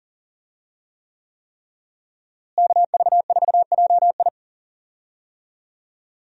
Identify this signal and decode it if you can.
Morse code, heard as a keyed tone, about 40 wpm, 700 Hz. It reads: KV4JI
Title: USA Call Signs - 5 Characters - Repeated 3x - 40wpm